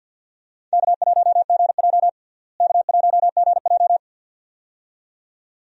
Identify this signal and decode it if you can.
Morse code, heard as a keyed tone, about 52 wpm, 700 Hz. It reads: K1GJ K1GJ
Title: USA Call Signs - 4 Characters - Repeated 2x - 50wpm